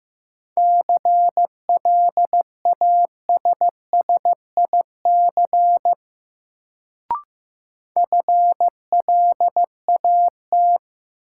Morse code, heard as a keyed tone, about 15 wpm, 700 Hz. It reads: CLASSIC FLAT
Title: Sentences from Top 1000 Words - New Word Drill - 15wpm